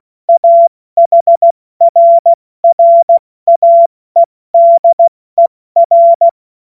Morse code, heard as a keyed tone, about 16 wpm, spaced as for 14 wpm, 680 Hz. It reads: AHRRAEDER
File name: Morsekurs / Witze / Witz 27 Mai 2025 70 F80